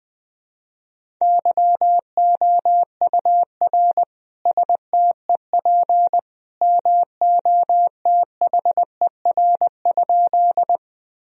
Morse code, heard as a keyed tone, about 20 wpm, 700 Hz. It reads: YOUR STEP MOTHER?
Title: Sentences from Top 1000 Words - Review Part 2 - 20wpm